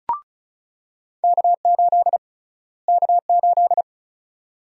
Morse code, heard as a keyed tone, about 34 wpm, 700 Hz. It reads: K8 K8
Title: USA Call Sign Prefixes - Repeated 2x - 35wpm